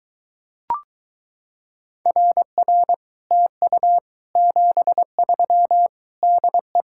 Morse code, heard as a keyed tone, about 23 wpm, 700 Hz. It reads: RR TU 73 DE